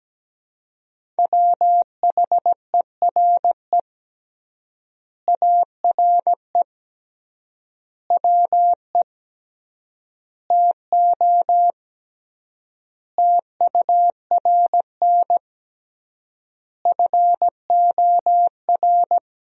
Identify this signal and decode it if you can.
Morse code, heard as a keyed tone, about 17 wpm, 700 Hz. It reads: WHERE ARE WE TO TURN FOR
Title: 3x Word-Spacing - Sentences from Top 200 Words - 17wpm